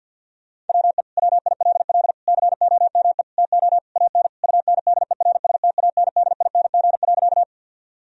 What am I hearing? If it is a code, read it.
Morse code, heard as a keyed tone, about 50 wpm, 700 Hz. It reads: WE WILL COME TO AN UNDERSTANDING.
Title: Sentences from Top 1000 Words - Review Part 3 - 50wpm